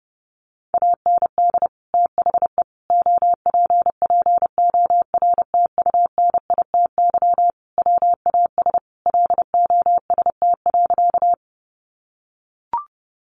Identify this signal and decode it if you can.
Morse code, heard as a keyed tone, about 30 wpm, 700 Hz. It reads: AND THE OPPORTUNITY WAS LOST.
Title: Sentences from Top 1400 Words 30wpm